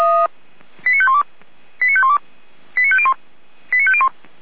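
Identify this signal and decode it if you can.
Five-tone selective call (ZVEI), heard as quick runs of five tones, tones of about 70 ms. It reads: 89512 89512 89501 89501
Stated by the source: ZVEI five-tone call 89135 received on SDR.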